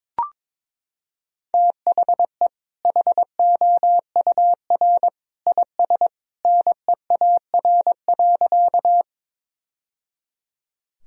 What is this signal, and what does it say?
Morse code, heard as a keyed tone, about 22 wpm, 700 Hz. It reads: THE HOUR IS NEAR.
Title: Sentences from Top 1000 Words - Review Part 1 - 22wpm